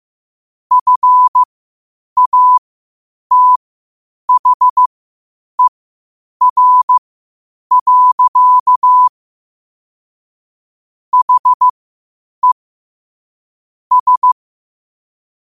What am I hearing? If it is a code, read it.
Morse code, heard as a keyed tone, about 15 wpm, spaced as for 8 wpm, 1000 Hz. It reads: FATHER. HE S
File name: qotdfiles / 8wpm